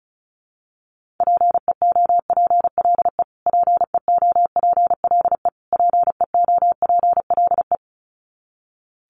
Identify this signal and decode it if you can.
Morse code, heard as a keyed tone, about 34 wpm, 700 Hz. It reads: PEOPLE PEOPLE PEOPLE
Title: Top 200 Words Repeated 35wpm